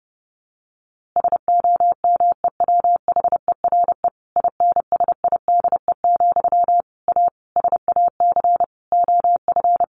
Morse code, heard as a keyed tone, about 30 wpm, 700 Hz. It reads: SOMEWHERE INSIDE, A SAC OF
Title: A Sound Of Thunder 30wpm